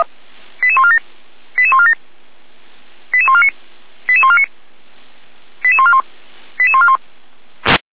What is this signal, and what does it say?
Five-tone selective call (ZVEI), heard as quick runs of five tones, tones of about 70 ms. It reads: 88147 88147 88149 88149 88141 88141